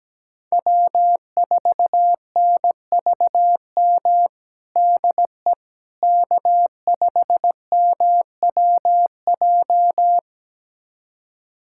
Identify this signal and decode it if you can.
Morse code, heard as a keyed tone, about 17 wpm, 700 Hz. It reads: W4NVM DE K5MWJ
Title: USA Callsign DE Callsign - 5 Characters - 17wpm